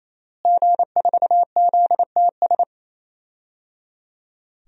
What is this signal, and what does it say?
Morse code, heard as a keyed tone, about 27 wpm, 700 Hz. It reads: G4ZTS
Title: DXCC-100 Call Signs - 5 Characters - Repeated 3x - 28wpm